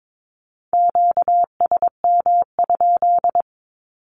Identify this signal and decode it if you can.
Morse code, heard as a keyed tone, about 22 wpm, 700 Hz. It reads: QSM?
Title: Q-Codes with Spoken Meaning - 22wpm